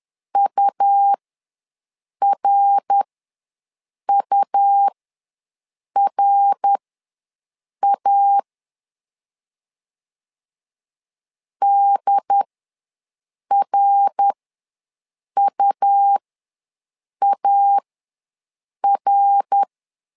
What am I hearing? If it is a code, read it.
Morse code, heard as a keyed tone, about 11 wpm, spaced as for 6 wpm, 800 Hz. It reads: URURA DRUAR
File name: LEZIONI / 3 MWRO / 3 MWR ws 10wpm 2151